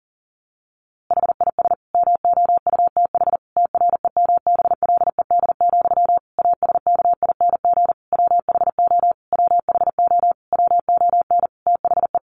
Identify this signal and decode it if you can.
Morse code, heard as a keyed tone, about 40 wpm, 700 Hz. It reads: HIS MOUTH TREMBLED, ASKING WHO WHO WON THE
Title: A Sound Of Thunder 40wpm